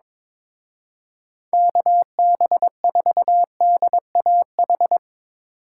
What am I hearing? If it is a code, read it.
Morse code, heard as a keyed tone, about 24 wpm, 700 Hz. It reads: KB4DAH
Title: Speed Racing - USA 2x3 Call Signs 22wpm